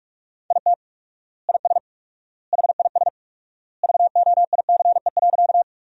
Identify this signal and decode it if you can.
Morse code, heard as a keyed tone, about 46 wpm, 700 Hz. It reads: IT IS HIS VOICE.
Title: 4x Word-Spacing - Sentences from Top 500 Words - 45wpm